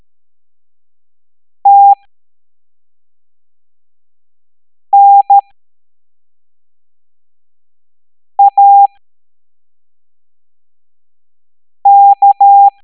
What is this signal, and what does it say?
Morse code, heard as a keyed tone, about 13 wpm, spaced as for 3 wpm, 800 Hz. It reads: TNAK
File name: hamdata / L01-EX02-ETANTK